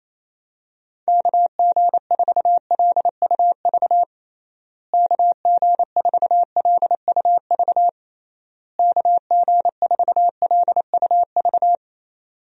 Morse code, heard as a keyed tone, about 28 wpm, 700 Hz. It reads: KG4LUV KG4LUV KG4LUV
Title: USA Call Signs - 6 Characters - Repeated 3x - 28wpm